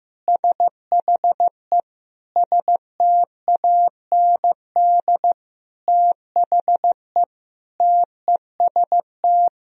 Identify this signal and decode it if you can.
Morse code, heard as a keyed tone, about 15 wpm, 700 Hz. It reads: SHE STAND THE TEST
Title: Sentences from Top 1000 Words - Review Part 3 - 15wpm